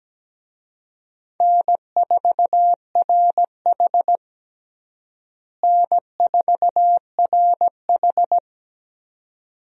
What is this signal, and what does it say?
Morse code, heard as a keyed tone, about 17 wpm, 700 Hz. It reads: N4RH N4RH
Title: USA Call Signs - 4 Characters - Repeated 2x - 17wpm